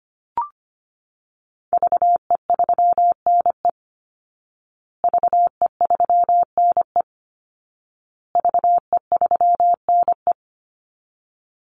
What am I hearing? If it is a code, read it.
Morse code, heard as a keyed tone, about 25 wpm, 700 Hz. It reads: VE3NE VE3NE VE3NE
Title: Call Signs Repeated 2x - 25wpm